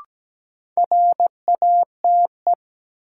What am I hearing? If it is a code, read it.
Morse code, heard as a keyed tone, about 17 wpm, 700 Hz. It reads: RATE
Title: Sentences from Top 700 Words - New Word Drill - 17wpm